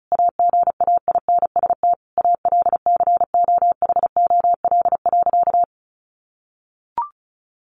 Morse code, heard as a keyed tone, about 34 wpm, 700 Hz. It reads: AGAINST ALCOHOL.
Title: Sentences from Top 1400 Words 35wpm